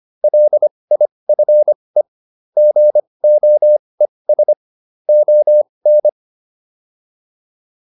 Morse code, heard as a keyed tone, about 25 wpm, spaced as for 20 wpm, 600 Hz. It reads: LIFE GOES ON